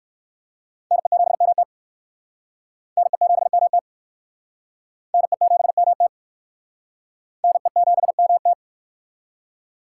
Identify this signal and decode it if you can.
Morse code, heard as a keyed tone, about 55 wpm, 700 Hz. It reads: NE7MT NE7MT NE7MT NE7MT
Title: Speed Racing - USA 2x2 Call Signs 45wpm